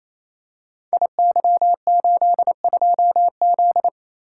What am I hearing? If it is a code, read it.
Morse code, heard as a keyed tone, about 27 wpm, 700 Hz. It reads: IY82Z
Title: Code Groups - 5 Letter:Numbers - 28wpm